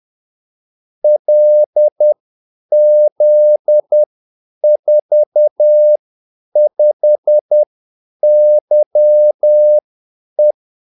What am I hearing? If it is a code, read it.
Morse code, heard as a keyed tone, about 10 wpm, spaced as for 8 wpm, 600 Hz. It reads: LZ45YE
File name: Callsigns-10-8wpm